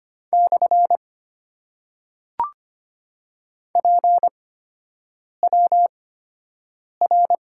Morse code, heard as a keyed tone, about 25 wpm, spaced as for 7 wpm, 700 Hz. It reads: / PWR
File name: Top 90 QSO Elements - 8x Character Spacing - 25wpm